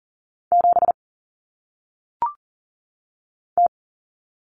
Morse code, heard as a keyed tone, about 38 wpm, 700 Hz. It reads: 7 T
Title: Common Cut Numbers - 1-Digit 40wpm